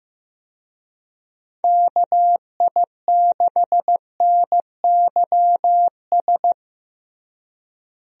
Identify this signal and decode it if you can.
Morse code, heard as a keyed tone, about 15 wpm, 700 Hz. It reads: KI6NYS
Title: USA Call Signs - 6 Characters - Repeated 3x - 15wpm